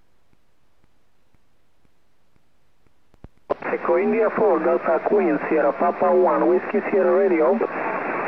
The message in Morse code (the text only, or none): none